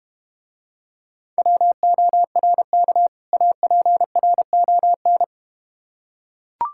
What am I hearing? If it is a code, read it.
Morse code, heard as a keyed tone, about 32 wpm, 700 Hz. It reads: WORK APRON